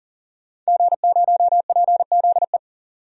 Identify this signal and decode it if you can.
Morse code, heard as a keyed tone, about 40 wpm, 700 Hz. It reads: G0PZE
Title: DXCC-100 Call Signs - 5 Characters - 40wpm